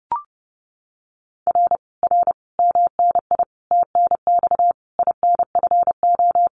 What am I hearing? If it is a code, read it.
Morse code, heard as a keyed tone, about 28 wpm, 700 Hz. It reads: R R MNI TNX INFO
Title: Common QSO Phrases Encore 30wpm